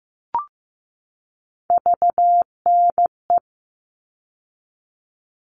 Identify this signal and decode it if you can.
Morse code, heard as a keyed tone, about 15 wpm, 700 Hz. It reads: VNE
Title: All Cut Numbers - 3-Digit 15wpm